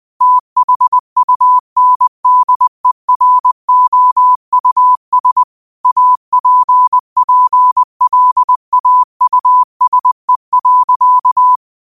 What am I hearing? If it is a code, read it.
Morse code, heard as a keyed tone, about 20 wpm, 1000 Hz. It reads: THUNDEROUS APPLAUSE.
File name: qotdfiles / 20wpm